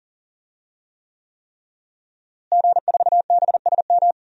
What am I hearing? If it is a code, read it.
Morse code, heard as a keyed tone, about 40 wpm, 700 Hz. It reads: G4BSM
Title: DXCC-100 Call Signs - 5 Characters - Repeated 3x - 40wpm